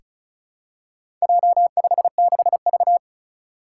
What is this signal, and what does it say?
Morse code, heard as a keyed tone, about 36 wpm, 700 Hz. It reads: J56V
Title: Four Challenging Letter-Number Combinations 35wpm